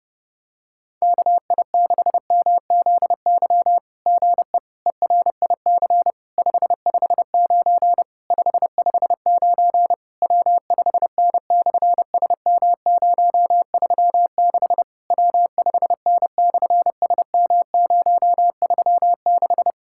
Morse code, heard as a keyed tone, about 30 wpm, 700 Hz. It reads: KI6MZY GE ERIC 559 559 W5N/SM036 W5N/SM036
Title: SOTA QSO Practice 30wpm